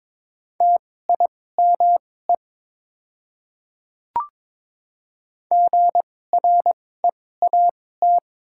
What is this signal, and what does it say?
Morse code, heard as a keyed tone, about 21 wpm, spaced as for 16 wpm, 700 Hz. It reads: TIME GREAT
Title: 2x Character Spacing - Top 200 Words - 22wpm